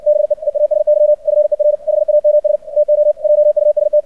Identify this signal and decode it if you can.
Morse code, heard as a keyed tone, about 59 wpm, 600 Hz. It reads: BE EASY FER HAMS TO JUST
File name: archived / CRSnet